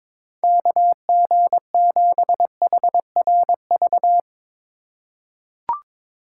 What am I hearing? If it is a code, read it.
Morse code, heard as a keyed tone, about 22 wpm, 700 Hz. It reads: KG7HRV
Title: USA Call Signs - 2x Character Spacing - 22wpm